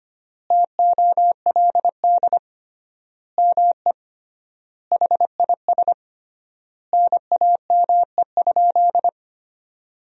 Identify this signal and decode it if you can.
Morse code, heard as a keyed tone, about 25 wpm, 700 Hz. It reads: TOLD ME HIS NAME?
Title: 3x Word-Spacing - Sentences from Top 400 Words - 25wpm